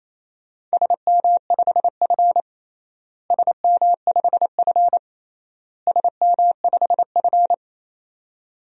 Morse code, heard as a keyed tone, about 28 wpm, 700 Hz. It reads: SM5F SM5F SM5F
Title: DXCC-100 Call Signs - 4 Characters - Repeated 3x - 28wpm